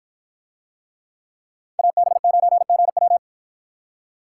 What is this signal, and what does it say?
Morse code, heard as a keyed tone, about 50 wpm, 700 Hz. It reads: AB9GW